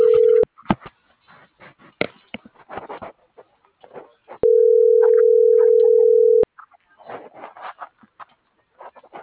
Ringback tone (North America), 440 + 480 Hz.